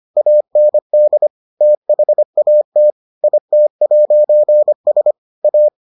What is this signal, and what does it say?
Morse code, heard as a keyed tone, about 25 wpm, 600 Hz. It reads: AND THAT IT'S A